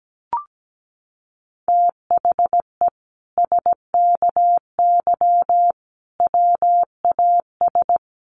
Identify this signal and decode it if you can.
Morse code, heard as a keyed tone, about 17 wpm, 700 Hz. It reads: THE SKY WAS
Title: Sentences from Top 1400 Words 17wpm